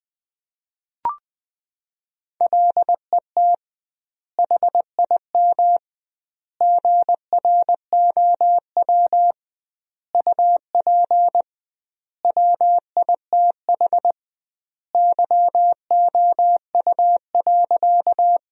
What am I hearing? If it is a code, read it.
Morse code, heard as a keyed tone, about 20 wpm, 700 Hz. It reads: LET HIM GROW UP WITH YOU.
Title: 2x Word-Spacing - Sentences from Top 300 Words - 20wpm